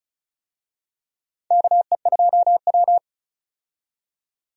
Morse code, heard as a keyed tone, about 36 wpm, 700 Hz. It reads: KE2W
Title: USA Call Signs - 4 Characters - 35wpm